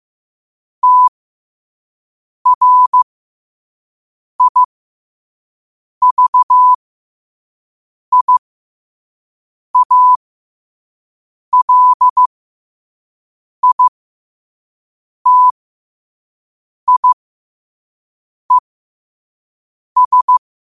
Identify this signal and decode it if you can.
Morse code, heard as a keyed tone, about 15 wpm, spaced as for 5 wpm, 1000 Hz. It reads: TRIVIALITIES
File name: qotdfiles / 5wpm